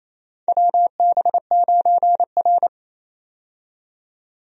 Morse code, heard as a keyed tone, about 27 wpm, 700 Hz. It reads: WB9R
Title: USA Call Signs - 4 Characters - 28wpm